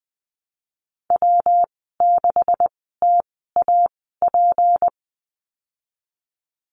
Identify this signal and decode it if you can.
Morse code, heard as a keyed tone, about 20 wpm, spaced as for 15 wpm, 700 Hz. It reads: W6TAP
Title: 193 Lesson 31g - Letter v - Call Signs 20wpm